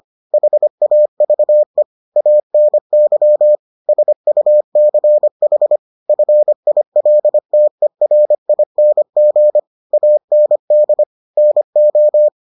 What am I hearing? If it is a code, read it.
Morse code, heard as a keyed tone, about 25 wpm, 600 Hz. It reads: HAVE ANY SUCH FILTERING AND NO